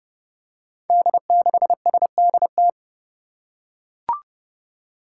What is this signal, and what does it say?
Morse code, heard as a keyed tone, about 30 wpm, 700 Hz. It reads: D6SDT
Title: Five Challenging Letter-Number Combinations 30wpm